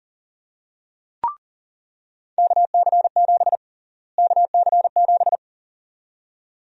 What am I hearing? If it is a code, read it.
Morse code, heard as a keyed tone, about 40 wpm, 700 Hz. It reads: KC7 KC7
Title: USA Call Sign Prefixes - Repeated 2x - 40wpm